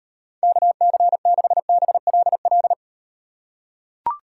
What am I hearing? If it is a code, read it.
Morse code, heard as a keyed tone, about 38 wpm, 700 Hz. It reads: KC6BLL